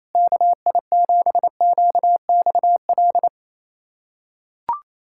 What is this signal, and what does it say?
Morse code, heard as a keyed tone, about 28 wpm, 700 Hz. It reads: KI7QXL
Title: USA Callsign DE Callsign - 6 Characters - 28wpm